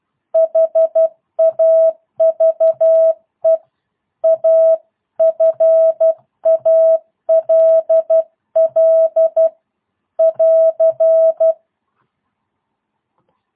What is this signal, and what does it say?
Morse code, heard as a keyed tone, about 12 wpm, 650 Hz. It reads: HAVE A FALL <AR>
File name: jh8cbh / 2022cw